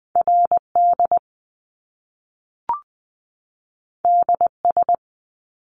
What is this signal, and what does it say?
Morse code, heard as a keyed tone, about 20 wpm, 700 Hz. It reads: RD DS
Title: Top Two Letter Combinations in Words 20wpm